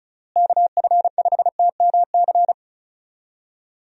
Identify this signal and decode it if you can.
Morse code, heard as a keyed tone, about 34 wpm, 700 Hz. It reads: KF5TMC
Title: USA Call Signs - 6 Characters - Repeated 3x - 35wpm